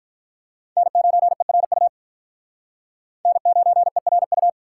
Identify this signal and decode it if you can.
Morse code, heard as a keyed tone, about 44 wpm, 700 Hz. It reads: N9ERU N9ERU